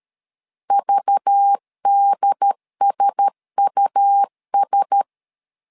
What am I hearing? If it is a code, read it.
Morse code, heard as a keyed tone, about 13 wpm, 800 Hz. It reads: VDSUS
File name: LEZIONI / 5 LBVC / 5 LBV 12wpm 1855